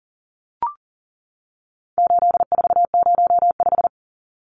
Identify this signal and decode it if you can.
Morse code, heard as a keyed tone, about 40 wpm, 700 Hz. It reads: 8405
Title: Numbers - 4-Digit 40wpm